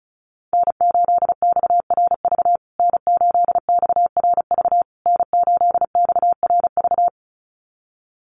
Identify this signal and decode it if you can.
Morse code, heard as a keyed tone, about 34 wpm, 700 Hz. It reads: N8XRV N8XRV N8XRV
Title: USA Call Signs Repeated 3x 35wpm